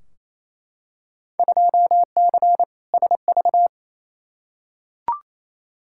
Morse code, heard as a keyed tone, about 28 wpm, 700 Hz. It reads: 2C SV
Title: Field Day Exchanges Encore - 28wpm